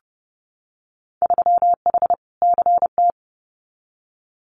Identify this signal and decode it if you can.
Morse code, heard as a keyed tone, about 30 wpm, 700 Hz. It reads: 3H CT
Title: Winter Field Day Exchanges Encore 30wpm